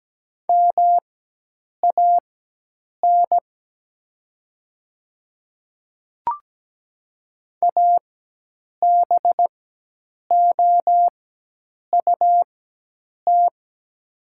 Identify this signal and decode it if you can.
Morse code, heard as a keyed tone, about 17 wpm, spaced as for 8 wpm, 700 Hz. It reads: MAN ABOUT
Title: 4x Character Spacing - Top 500 Words - 17wpm